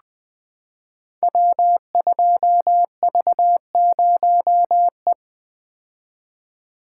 Morse code, heard as a keyed tone, about 20 wpm, 700 Hz. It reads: W2V0E
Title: Code Groups - 5 Letter:Numbers - 20wpm